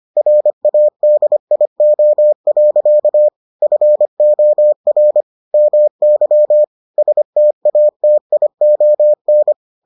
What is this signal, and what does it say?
Morse code, heard as a keyed tone, about 25 wpm, 600 Hz. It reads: RADIO. FOR MY STATION